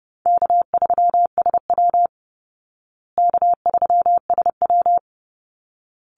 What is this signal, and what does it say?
Morse code, heard as a keyed tone, about 30 wpm, 700 Hz. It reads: K3SW K3SW
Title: Easy Call Signs Repeated 2x - 30wpm